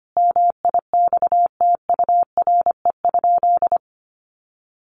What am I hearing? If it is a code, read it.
Morse code, heard as a keyed tone, about 25 wpm, 700 Hz. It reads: MIXTURE?
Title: Sentences from Top 1500 Words 25wpm